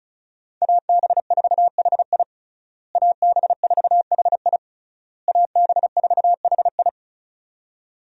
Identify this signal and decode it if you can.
Morse code, heard as a keyed tone, about 36 wpm, 700 Hz. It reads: AB4HI AB4HI AB4HI
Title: USA Call Signs - 5 Characters - Repeated 3x - 35wpm